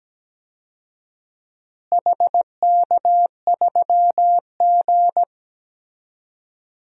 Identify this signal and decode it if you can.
Morse code, heard as a keyed tone, about 17 wpm, 700 Hz. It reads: HK3G
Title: DXCC-100 Call Signs - 4 Characters - Repeated 3x - 17wpm